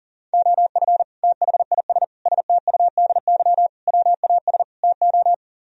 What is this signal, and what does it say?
Morse code, heard as a keyed tone, about 52 wpm, 700 Hz. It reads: OF THIS STUDY WAS TO